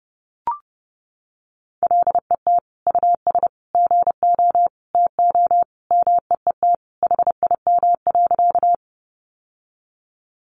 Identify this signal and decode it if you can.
Morse code, heard as a keyed tone, about 30 wpm, 700 Hz. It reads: LET US GO TO MEET HIM.
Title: Sentences from Top 1100 Words 30wpm